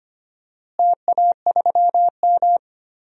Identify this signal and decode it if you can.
Morse code, heard as a keyed tone, about 25 wpm, 700 Hz. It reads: TA3M
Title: DXCC-100 Call Signs - 4 Characters - Repeated 3x - 25wpm